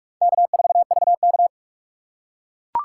Morse code, heard as a keyed tone, about 43 wpm, 700 Hz. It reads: K4VK
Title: USA Call Signs - 4x Character Spacing - 45wpm